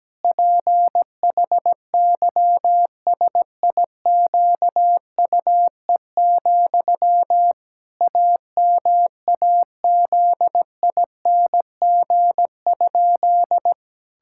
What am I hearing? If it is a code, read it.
Morse code, heard as a keyed tone, about 17 wpm, 700 Hz. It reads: PHYSIQUE, AMAZING?
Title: Pangram Challenge v2 - 17wpm